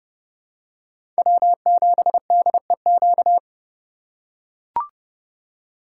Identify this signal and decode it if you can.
Morse code, heard as a keyed tone, about 31 wpm, 700 Hz. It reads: W7DEQ
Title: USA Call Signs - 3x Character Spacing - 30wpm